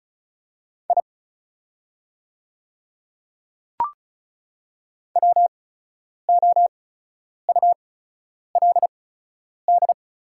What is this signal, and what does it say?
Morse code, heard as a keyed tone, about 36 wpm, spaced as for 10 wpm, 700 Hz. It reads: I WOULD
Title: 8x Character Spacing - Top 100 Words - 35wpm